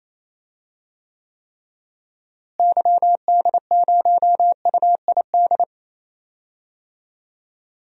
Morse code, heard as a keyed tone, about 28 wpm, 700 Hz. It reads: YD0UID
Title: DXCC-100 Call Signs - 6 Characters - Repeated 3x - 28wpm